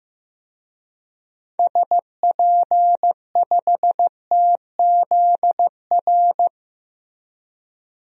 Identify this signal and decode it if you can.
Morse code, heard as a keyed tone, about 15 wpm, 700 Hz. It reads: SP5TZR
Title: DXCC-100 Call Signs - 6 Characters - Repeated 3x - 15wpm